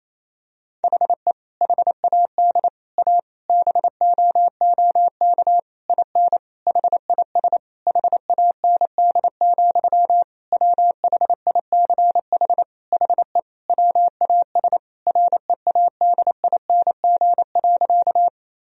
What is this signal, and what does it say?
Morse code, heard as a keyed tone, about 28 wpm, 700 Hz. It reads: HE HAD A BOOK IN HIS HAND, WHICH HE WAS READING.
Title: Sentences from Top 1000 Words - Review Part 3 - 28wpm